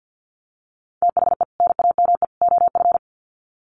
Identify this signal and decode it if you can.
Morse code, heard as a keyed tone, about 50 wpm, 700 Hz. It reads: THE NAME OF
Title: Sets of 3 Words 50wpm